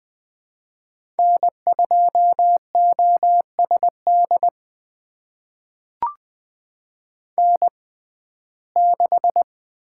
Morse code, heard as a keyed tone, about 20 wpm, spaced as for 14 wpm, 700 Hz. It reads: N2OSD N6
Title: USA Call Signs - 6x Character Spacing - 20wpm